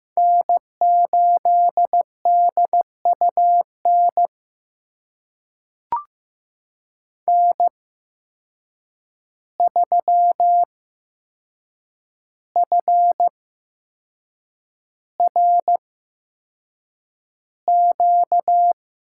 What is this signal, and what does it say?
Morse code, heard as a keyed tone, about 15 wpm, spaced as for 6 wpm, 700 Hz. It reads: N8DUN N3FRQ
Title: USA Call Signs - 5 Characters - 8x Character Spacing - 15wpm